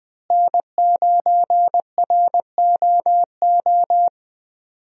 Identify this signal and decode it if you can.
Morse code, heard as a keyed tone, about 20 wpm, 700 Hz. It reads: N9ROO
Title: USA Call Signs - 5 Characters - Repeated 3x - 20wpm